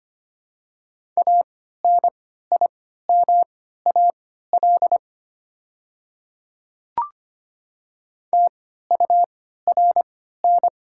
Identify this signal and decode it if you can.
Morse code, heard as a keyed tone, about 25 wpm, spaced as for 14 wpm, 690 Hz. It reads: ANIMAL TURN
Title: 3x Character Spacing - Top 500 Words - 25wpm